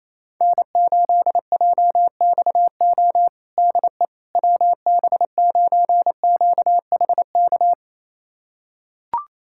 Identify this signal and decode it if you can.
Morse code, heard as a keyed tone, about 28 wpm, 700 Hz. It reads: N8JXO DE WB9QHK
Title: USA Callsign DE Callsign 28wpm